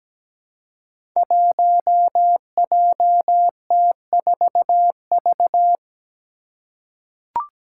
Morse code, heard as a keyed tone, about 17 wpm, 700 Hz. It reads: 1JT4V
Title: Five Challenging Letter-Number Combinations 17wpm